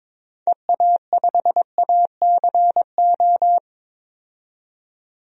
Morse code, heard as a keyed tone, about 22 wpm, 700 Hz. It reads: EA5ACO
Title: DXCC-100 Call Signs - 6 Characters - 22wpm